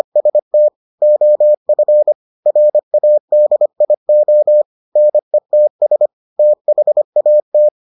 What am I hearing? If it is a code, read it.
Morse code, heard as a keyed tone, about 25 wpm, 600 Hz. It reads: ST OF RADIO NETS THAT